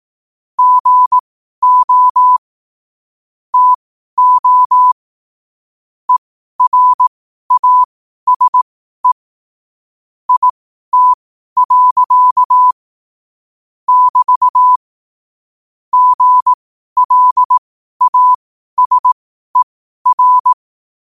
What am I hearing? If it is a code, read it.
Morse code, heard as a keyed tone, about 18 wpm, spaced as for 12 wpm, 1000 Hz. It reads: GO TO ERASE IT. <BT> GLASER
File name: qotdfiles / 12wpm